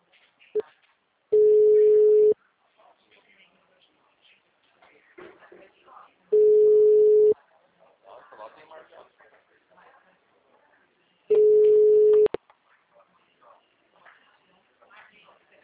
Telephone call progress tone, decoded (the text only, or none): ringback tone